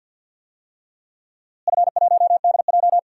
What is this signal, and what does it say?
Morse code, heard as a keyed tone, about 48 wpm, 700 Hz. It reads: F1DJ
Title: DXCC-100 Call Signs - 4 Characters - Repeated 3x - 50wpm